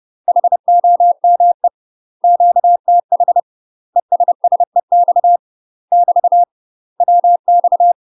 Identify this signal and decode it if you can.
Morse code, heard as a keyed tone, about 30 wpm, 700 Hz. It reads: HOME QTH ESSEX <BT> WX